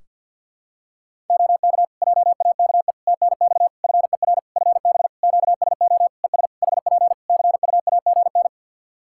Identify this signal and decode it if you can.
Morse code, heard as a keyed tone, about 51 wpm, 700 Hz. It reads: OK JAKE TNX FER FB QSO ES HP CUAGN
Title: Practice QSO - Exchange 4 - Ending - 50wpm